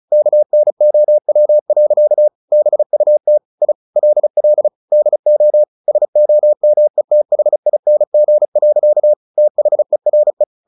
Morse code, heard as a keyed tone, about 34 wpm, 600 Hz. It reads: KNOW. BUT I LL DO SOMETHING. THERE
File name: the old man and the sea / the old man and the sea 1 35wpm